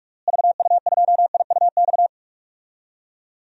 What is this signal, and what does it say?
Morse code, heard as a keyed tone, about 46 wpm, 700 Hz. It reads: VU2IUX
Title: DXCC-100 Call Signs - 6 Characters - Repeated 2x - 45wpm